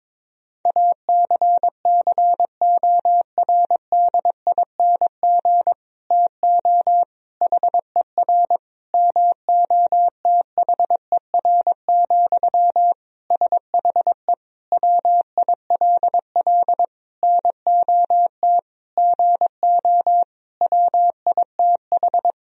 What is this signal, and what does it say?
Morse code, heard as a keyed tone, about 22 wpm, 700 Hz. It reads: ACCORDING TO HER MOTHER, SHE WILL NOT GO WITH
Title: Sentences from Top 1000 Words - Review Part 4 - 22wpm